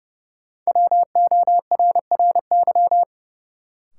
Morse code, heard as a keyed tone, about 30 wpm, 700 Hz. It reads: WORRY
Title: Sentences from Top 1100 Words - New Word Drill - 30wpm